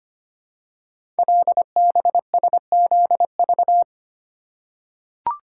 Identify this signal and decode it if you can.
Morse code, heard as a keyed tone, about 25 wpm, 700 Hz. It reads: LBSZV
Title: Code Groups - 5 Letters - 25wpm